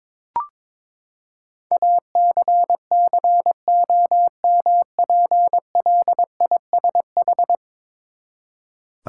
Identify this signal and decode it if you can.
Morse code, heard as a keyed tone, about 22 wpm, 700 Hz. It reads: ACCOMPLISH
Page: morse-code-ninja-practice-22wpm